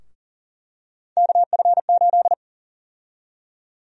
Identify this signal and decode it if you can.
Morse code, heard as a keyed tone, about 37 wpm, 700 Hz. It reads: KF8
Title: USA Call Sign Prefixes - Repeated 2x - 40wpm